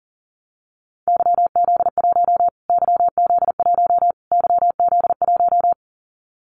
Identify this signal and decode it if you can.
Morse code, heard as a keyed tone, about 40 wpm, 700 Hz. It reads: YZ1 YZ1 YZ1
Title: Callsign Prefixes for Top 25 DXCC - Repeated 3x - 40wpm